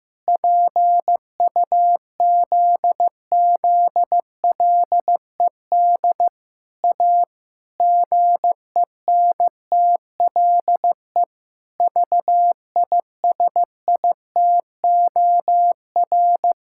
Morse code, heard as a keyed tone, about 15 wpm, 700 Hz. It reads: PUZZLED A GENTLE VISITOR
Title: Pangram Challenge v2 - 15wpm